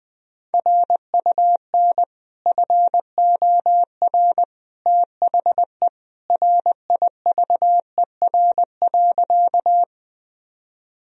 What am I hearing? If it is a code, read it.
Morse code, heard as a keyed tone, about 20 wpm, 700 Hz. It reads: RUN FOR THE RIVER.
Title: Sentences from Top 1000 Words - Review Part 1 - 20wpm